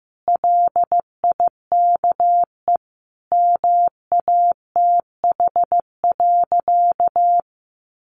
Morse code, heard as a keyed tone, about 15 wpm, 700 Hz. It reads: LIKE MATH.
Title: Easy Phrases 15wpm